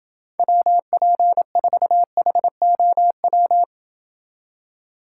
Morse code, heard as a keyed tone, about 28 wpm, 700 Hz. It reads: WP4HOW